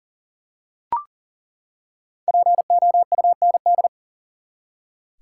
Morse code, heard as a keyed tone, about 40 wpm, 700 Hz. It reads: POUND
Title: Five Letter Words - 40wpm